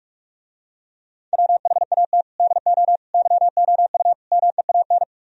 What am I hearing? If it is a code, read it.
Morse code, heard as a keyed tone, about 48 wpm, 700 Hz. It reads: WHAT DO YOU MEAN